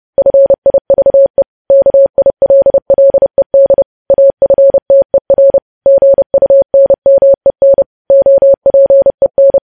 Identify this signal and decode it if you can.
Morse code, heard as a keyed tone, about 30 wpm, 550 Hz. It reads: FIVE KILLED AFTER GUNMEN OPEN
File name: podcast / News headlines-1x-30x30